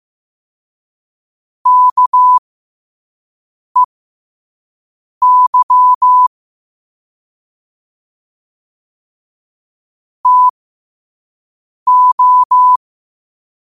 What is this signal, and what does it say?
Morse code, heard as a keyed tone, about 15 wpm, spaced as for 5 wpm, 1000 Hz. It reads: KEY TO
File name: 5wpm